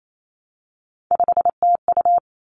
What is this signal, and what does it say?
Morse code, heard as a keyed tone, about 28 wpm, 700 Hz. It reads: 5TU
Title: Three Challenging Letter-Number Combinations 28wpm